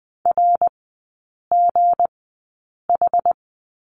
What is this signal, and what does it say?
Morse code, heard as a keyed tone, about 20 wpm, spaced as for 8 wpm, 700 Hz. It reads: RGH
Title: 184 Lesson 30e - Letter b - Three Characters 20wpm